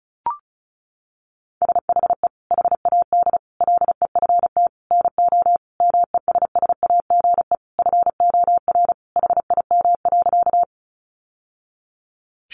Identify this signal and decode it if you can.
Morse code, heard as a keyed tone, about 34 wpm, 700 Hz. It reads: SHE HAD LEFT NO MESSAGE FOR HIM.
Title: Sentences from Top 1100 Words 35wpm